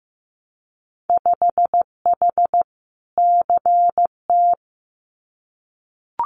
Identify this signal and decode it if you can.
Morse code, heard as a keyed tone, about 15 wpm, 700 Hz. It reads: 5H CT
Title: Winter Field Day Exchanges Encore 15wpm